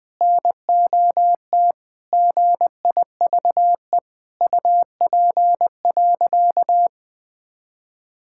Morse code, heard as a keyed tone, about 20 wpm, 700 Hz. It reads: NOT GIVE UP.
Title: Sentences with 3-Word Phrases - 20wpm